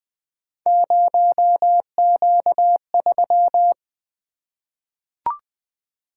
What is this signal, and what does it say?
Morse code, heard as a keyed tone, about 20 wpm, 700 Hz. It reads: 0Q3
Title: Three Letter-Number Combinations - 20wpm